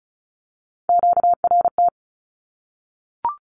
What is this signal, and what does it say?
Morse code, heard as a keyed tone, about 34 wpm, 700 Hz. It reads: QRT
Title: Q-Codes Repeated 3x 35wpm